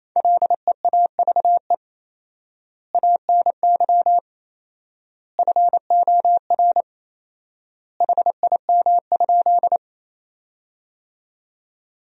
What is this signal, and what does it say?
Morse code, heard as a keyed tone, about 28 wpm, 700 Hz. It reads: LEAVE ANY FOR HIM?
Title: 4x Word-Spacing - Sentences from Top 400 Words - 28wpm